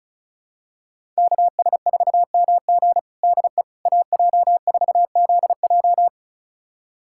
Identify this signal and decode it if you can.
Morse code, heard as a keyed tone, about 36 wpm, 700 Hz. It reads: KS4MG DE AJ4ZJ
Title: USA Callsign DE Callsign - 5 Characters - 35wpm